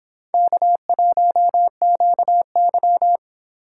Morse code, heard as a keyed tone, about 29 wpm, 700 Hz. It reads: K1QY